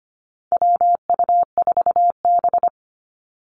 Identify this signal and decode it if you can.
Morse code, heard as a keyed tone, about 25 wpm, 700 Hz. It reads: WU4B
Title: Call Signs - 25wpm